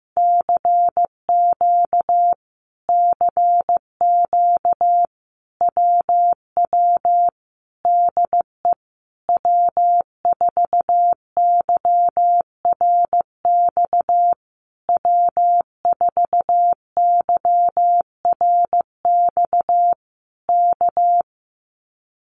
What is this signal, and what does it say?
Morse code, heard as a keyed tone, about 15 wpm, 700 Hz. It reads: CQ CQ WW DE W4YRX W4YRX K
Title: Common QSO Phrases Encore 15wpm